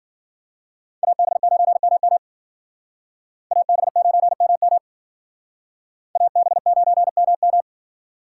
Morse code, heard as a keyed tone, about 50 wpm, 700 Hz. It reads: AB9MM AB9MM AB9MM
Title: Speed Racing - USA Call Signs 40wpm